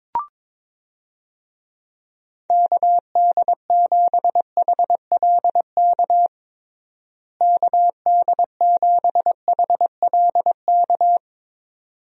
Morse code, heard as a keyed tone, about 22 wpm, 700 Hz. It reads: KD7HLK KD7HLK
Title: USA Call Signs - 6 Characters - Repeated 2x - 22wpm